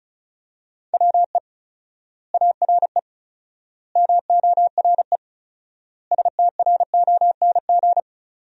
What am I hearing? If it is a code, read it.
Morse code, heard as a keyed tone, about 36 wpm, 700 Hz. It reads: WE ARE MORE STRONG
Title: 4x Word-Spacing - Sentences from Top 500 Words - 35wpm